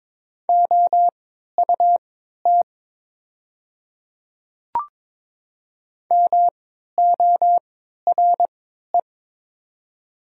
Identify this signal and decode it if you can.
Morse code, heard as a keyed tone, about 22 wpm, spaced as for 12 wpm, 700 Hz. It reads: OUT MORE
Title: 3x Character Spacing - Top 100 Words - 22wpm